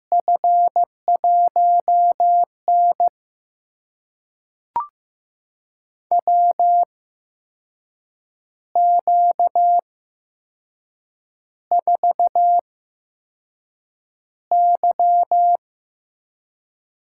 Morse code, heard as a keyed tone, about 15 wpm, spaced as for 6 wpm, 700 Hz. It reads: F1N WQ4Y
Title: USA Call Signs - 4 Characters - 8x Character Spacing - 15wpm